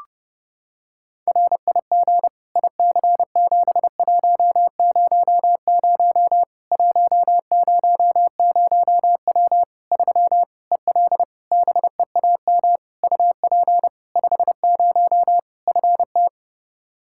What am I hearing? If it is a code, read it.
Morse code, heard as a keyed tone, about 30 wpm, 700 Hz. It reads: RIG IC7100 100W 3 EL BEAM UP 50 FT
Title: Practice QSO - Exchange 2 - Info - 30wpm